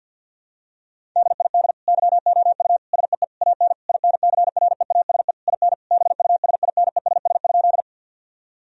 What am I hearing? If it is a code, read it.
Morse code, heard as a keyed tone, about 50 wpm, 700 Hz. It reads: DID YOU SEE AN INCREASE IN BUSINESS?
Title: Sentences from Top 1000 Words - Review Part 4 - 50wpm